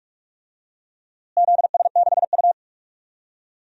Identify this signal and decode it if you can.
Morse code, heard as a keyed tone, about 44 wpm, 700 Hz. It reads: ZS6U
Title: DXCC-100 Call Signs - 4 Characters - Repeated 3x - 45wpm